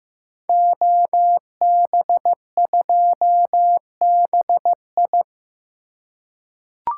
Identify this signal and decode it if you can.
Morse code, heard as a keyed tone, about 15 wpm, 700 Hz. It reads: OB2BI
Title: Code Groups - 5 Letter:Numbers - 15wpm